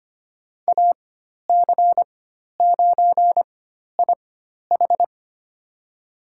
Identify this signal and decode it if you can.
Morse code, heard as a keyed tone, about 25 wpm, spaced as for 12 wpm, 700 Hz. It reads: AC9IH
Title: USA Call Signs - 4x Character Spacing - 25wpm